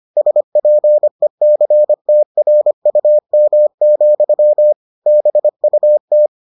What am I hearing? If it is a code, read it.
Morse code, heard as a keyed tone, about 25 wpm, 600 Hz. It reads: SPECTRUM, BUT